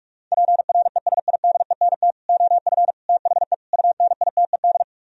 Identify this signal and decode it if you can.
Morse code, heard as a keyed tone, about 44 wpm, 700 Hz. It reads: PRESIDENT OF THE UNITED